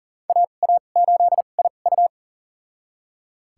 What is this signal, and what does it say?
Morse code, heard as a keyed tone, about 40 wpm, spaced as for 29 wpm, 700 Hz. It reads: AA8IU
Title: USA Call Signs - 5 Characters - 2x Character Spacing - 40wpm